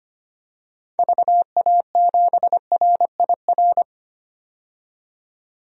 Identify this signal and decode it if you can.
Morse code, heard as a keyed tone, about 25 wpm, 700 Hz. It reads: VA7RIR
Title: DXCC-100 Call Signs - 6 Characters - 25wpm